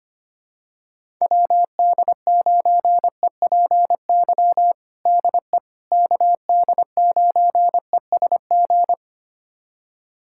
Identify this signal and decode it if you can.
Morse code, heard as a keyed tone, about 25 wpm, 700 Hz. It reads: WD9EPY DE KD9ESG
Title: USA Callsign DE Callsign 25wpm